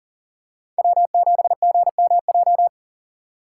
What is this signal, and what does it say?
Morse code, heard as a keyed tone, about 40 wpm, 700 Hz. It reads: W7GMJ
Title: USA Call Signs - 5 Characters - 4x Character Spacing - 40wpm